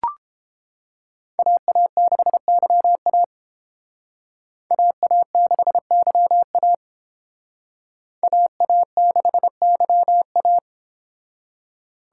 Morse code, heard as a keyed tone, about 32 wpm, 700 Hz. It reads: AA6YA AA6YA AA6YA